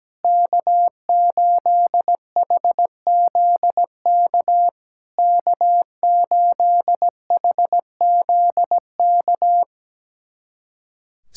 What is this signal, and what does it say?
Morse code, heard as a keyed tone, about 17 wpm, 700 Hz. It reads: K8HZK K8HZK
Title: SOTA QSO Practice 17wpm